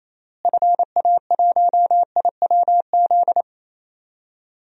Morse code, heard as a keyed tone, about 28 wpm, 700 Hz. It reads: FA1IWZ
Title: DXCC-100 Call Signs - 6 Characters - Repeated 3x - 28wpm